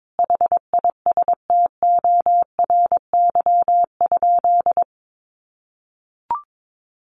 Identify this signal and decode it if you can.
Morse code, heard as a keyed tone, about 22 wpm, 700 Hz. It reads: HISTORY?
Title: Sentences from Top 600 Words 22wpm